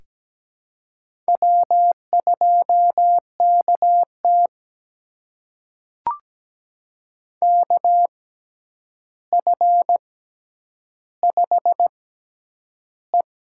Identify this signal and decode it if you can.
Morse code, heard as a keyed tone, about 17 wpm, spaced as for 9 wpm, 700 Hz. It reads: W2KT KF5E
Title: USA Call Signs - 4 Characters - 6x Character Spacing - 17wpm